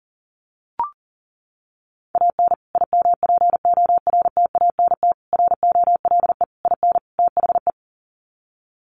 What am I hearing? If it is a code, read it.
Morse code, heard as a keyed tone, about 40 wpm, 700 Hz. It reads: AN IMPORTANT ROLE IN THE
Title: Sets of 5 Words Encore2 40wpm